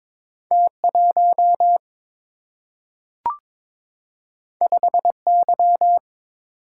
Morse code, heard as a keyed tone, about 22 wpm, 700 Hz. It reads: T1 5Y
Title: Single-Letter Single-Number 22wpm